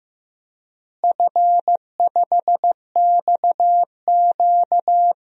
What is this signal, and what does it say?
Morse code, heard as a keyed tone, about 16 wpm, 700 Hz. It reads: F5XQ
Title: DXCC-100 Call Signs - 4 Characters - Repeated 3x - 15wpm